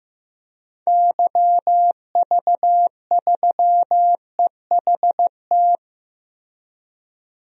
Morse code, heard as a keyed tone, about 16 wpm, 700 Hz. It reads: YV3EHT
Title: DXCC-100 Call Signs - 6 Characters - Repeated 3x - 15wpm